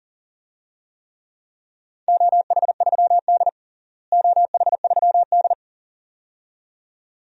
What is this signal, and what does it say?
Morse code, heard as a keyed tone, about 37 wpm, 700 Hz. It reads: OH3D OH3D
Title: DXCC-100 Call Signs - 4 Characters - Repeated 2x - 40wpm